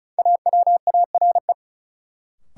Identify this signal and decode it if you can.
Morse code, heard as a keyed tone, about 34 wpm, 700 Hz. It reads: AWARE
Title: Sentences from Top 1200 Words - New Word Drill - 35wpm